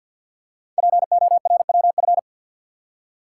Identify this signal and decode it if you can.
Morse code, heard as a keyed tone, about 50 wpm, 700 Hz. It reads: PORWF
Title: Code Groups - 5 Letters - 50wpm